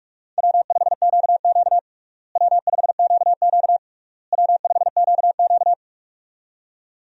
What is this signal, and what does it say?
Morse code, heard as a keyed tone, about 46 wpm, 700 Hz. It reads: W5QQ W5QQ W5QQ
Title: USA Call Signs - 4 Characters - Repeated 3x - 45wpm